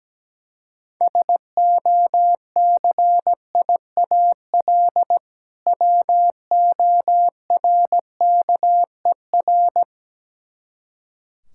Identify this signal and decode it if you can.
Morse code, heard as a keyed tone, about 17 wpm, 700 Hz. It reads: SOCIAL WORKER
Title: Practice QSO - Exchange 3 - More Info - 17wpm